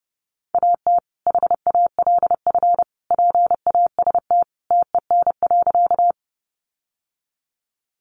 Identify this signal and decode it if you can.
Morse code, heard as a keyed tone, about 30 wpm, 700 Hz. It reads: AT HALF PAST TEN.
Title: Sentences from Top 600 Words 30wpm